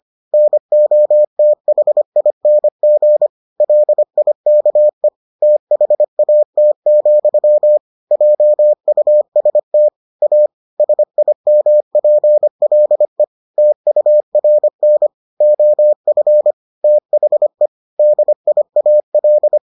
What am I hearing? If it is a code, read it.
Morse code, heard as a keyed tone, about 25 wpm, 600 Hz. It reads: NOTHING LIKE THAT, JUST A SIMPLE TURN OF THE DIAL